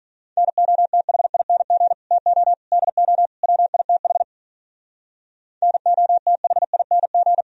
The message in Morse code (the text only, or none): NOTHING TO DO WITH NOTHING